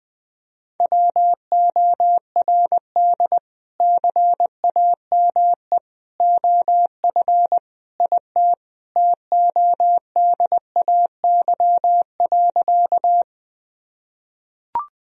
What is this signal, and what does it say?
Morse code, heard as a keyed tone, about 20 wpm, 700 Hz. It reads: WORD CAME OF IT TODAY.
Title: Sentences from Top 1000 Words - Review Part 4 - 20wpm